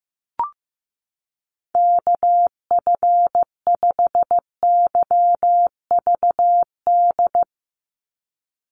Morse code, heard as a keyed tone, about 15 wpm, 700 Hz. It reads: KF5YVD
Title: USA Call Signs 15wpm